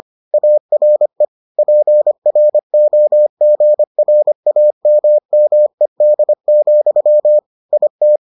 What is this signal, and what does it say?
Morse code, heard as a keyed tone, about 25 wpm, 600 Hz. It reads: ARE PROGRAMMED, IT